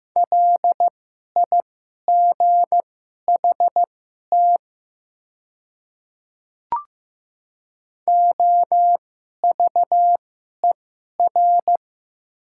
Morse code, heard as a keyed tone, about 15 wpm, spaced as for 11 wpm, 700 Hz. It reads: LIGHT OVER
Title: 2x Character Spacing - Top 200 Words - 15wpm